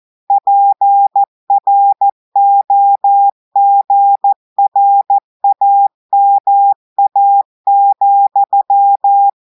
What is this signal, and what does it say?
Morse code, heard as a keyed tone, about 14 wpm, 800 Hz. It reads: PROGRAMA,
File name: audios / portugais / portugais texte en clair 2 vitesse 14 mots minutes